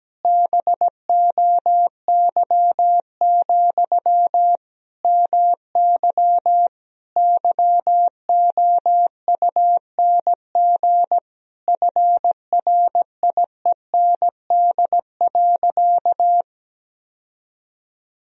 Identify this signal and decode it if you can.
Morse code, heard as a keyed tone, about 17 wpm, 700 Hz. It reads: BOY, MY YOUNG FRIEND.
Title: Sentences from Top 1000 Words - Review Part 3 - 17wpm